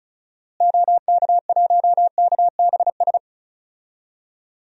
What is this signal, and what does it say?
Morse code, heard as a keyed tone, about 36 wpm, 700 Hz. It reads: OK1KBS
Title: DXCC-100 Call Signs - 6 Characters - Repeated 2x - 35wpm